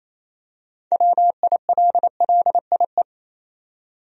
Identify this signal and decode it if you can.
Morse code, heard as a keyed tone, about 28 wpm, 700 Hz. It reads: WILLIE
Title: USA Common Names - 28wpm